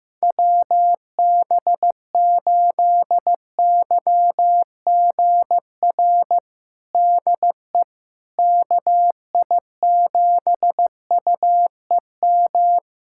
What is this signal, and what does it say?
Morse code, heard as a keyed tone, about 15 wpm, 700 Hz. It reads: WB8YGR DE KI7UEM
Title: USA Callsign DE Callsign - 6 Characters - 15wpm